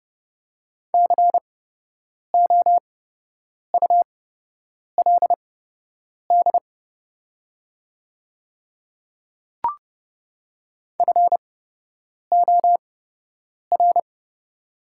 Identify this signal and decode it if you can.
Morse code, heard as a keyed tone, about 30 wpm, spaced as for 8 wpm, 700 Hz. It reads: COULD FOR
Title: 8x Character Spacing - Top 100 Words - 30wpm